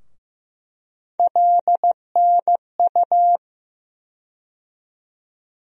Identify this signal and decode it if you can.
Morse code, heard as a keyed tone, about 15 wpm, 700 Hz. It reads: LNU
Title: USA Call Sign Postfixes - 3 Letters 15wpm